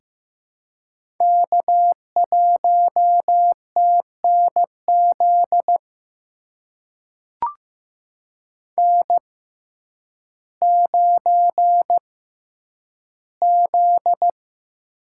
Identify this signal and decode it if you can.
Morse code, heard as a keyed tone, about 15 wpm, spaced as for 9 wpm, 700 Hz. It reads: K1TNZ N9Z
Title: USA Call Signs - 5 Characters - 6x Character Spacing - 15wpm